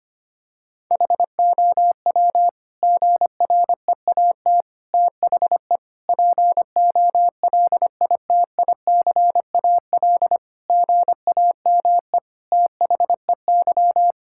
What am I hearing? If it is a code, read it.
Morse code, heard as a keyed tone, about 25 wpm, 700 Hz. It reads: HOW GREAT THE POLITICAL GAME THEY
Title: Sentences from Top 1000 Words - Review Part 5 - 25wpm